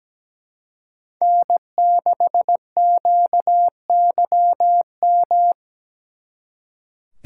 Morse code, heard as a keyed tone, about 20 wpm, 700 Hz. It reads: N6QYM
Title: Speed Racing - USA Call Signs 17wpm